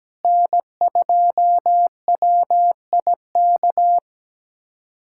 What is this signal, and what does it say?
Morse code, heard as a keyed tone, about 17 wpm, 700 Hz. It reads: N2WIK
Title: USA Callsign DE Callsign - 5 Characters - 17wpm